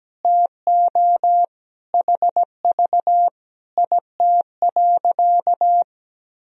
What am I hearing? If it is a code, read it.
Morse code, heard as a keyed tone, about 17 wpm, 700 Hz. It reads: TO HV IT.
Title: Sentences from Top 100 Words and Abbreviations 17wpm